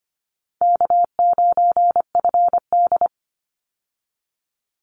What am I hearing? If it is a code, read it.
Morse code, heard as a keyed tone, about 25 wpm, 700 Hz. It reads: K9FD
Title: Easy Call Signs - 25wpm